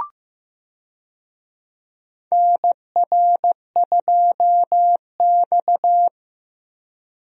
Morse code, heard as a keyed tone, about 15 wpm, 700 Hz. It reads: NR2X
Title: USA Call Signs - 4 Characters - 15wpm